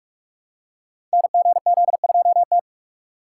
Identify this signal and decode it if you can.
Morse code, heard as a keyed tone, about 42 wpm, 700 Hz. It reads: NGZ2T
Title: Code Groups - 5 Letter:Numbers - 45wpm